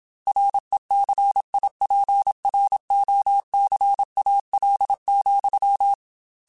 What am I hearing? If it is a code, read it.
Morse code, heard as a keyed tone, about 27 wpm, 800 Hz. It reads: RECIPROCAL,
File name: Technical Subjects / Technical Subjects-mp3s / XC @25